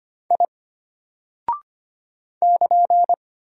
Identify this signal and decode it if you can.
Morse code, heard as a keyed tone, about 25 wpm, 700 Hz. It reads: I <KN>
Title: Single Character (Letter:Number:Punctuation:Prosign) Alt 25wpm